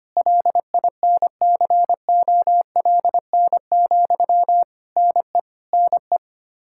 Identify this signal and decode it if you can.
Morse code, heard as a keyed tone, about 25 wpm, 700 Hz. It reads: LINCOLN, NE NE
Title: Practice QSO - Exchange 1 - Beginning - 25wpm